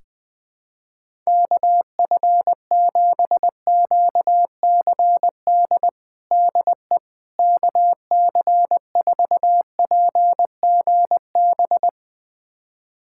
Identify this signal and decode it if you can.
Morse code, heard as a keyed tone, about 20 wpm, 700 Hz. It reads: KF7QCD DE KC4PGB
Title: USA Callsign DE Callsign 20wpm